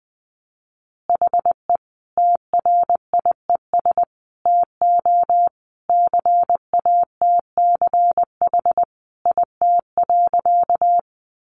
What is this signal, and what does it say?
Morse code, heard as a keyed tone, about 20 wpm, 700 Hz. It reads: HE TRIES TO CATCH IT.
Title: Additional Easy Phrases 20wpm